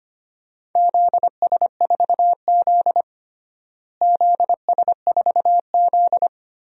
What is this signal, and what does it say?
Morse code, heard as a keyed tone, about 25 wpm, 700 Hz. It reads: ZS4Z ZS4Z
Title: DXCC-100 Call Signs - 4 Characters - Repeated 2x - 25wpm